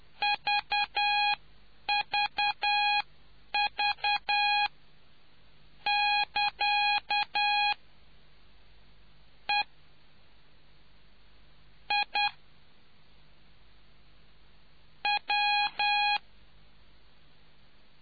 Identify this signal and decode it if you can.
Morse code, heard as a keyed tone, about 10 wpm, spaced as for 8 wpm, 820 Hz. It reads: VVV <KA> E I W